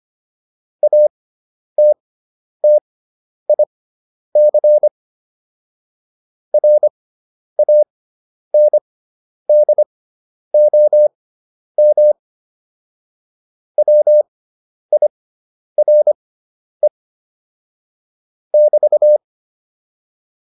Morse code, heard as a keyed tone, about 25 wpm, spaced as for 10 wpm, 600 Hz. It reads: ATTIC RANDOM WIRE <BT>
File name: QSO206 10